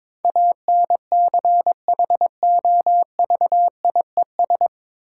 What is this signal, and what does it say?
Morse code, heard as a keyed tone, about 22 wpm, 690 Hz. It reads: ANCHOVIES
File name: Difficult Words with Hints - 22wpm